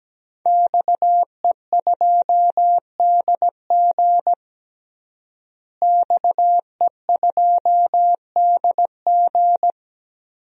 Morse code, heard as a keyed tone, about 17 wpm, 700 Hz. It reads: XE2DG XE2DG
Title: DXCC-100 Call Signs - 5 Characters - Repeated 2x - 17wpm